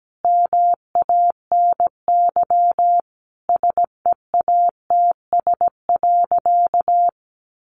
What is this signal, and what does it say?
Morse code, heard as a keyed tone, about 17 wpm, 700 Hz. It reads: MANY SEATS.
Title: Additional Easy Phrases 17wpm